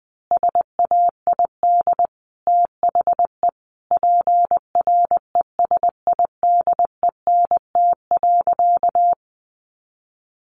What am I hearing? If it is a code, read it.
Morse code, heard as a keyed tone, about 20 wpm, 700 Hz. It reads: SAID THE PRESIDENT.
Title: Sentences from Top 1000 Words 20wpm